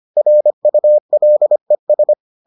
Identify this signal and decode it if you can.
Morse code, heard as a keyed tone, about 25 wpm, 600 Hz. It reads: RULES